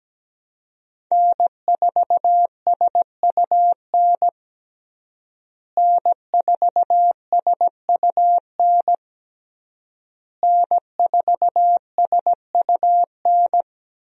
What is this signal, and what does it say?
Morse code, heard as a keyed tone, about 17 wpm, 700 Hz. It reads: N4SUN N4SUN N4SUN
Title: USA Call Signs - 5 Characters - Repeated 3x - 17wpm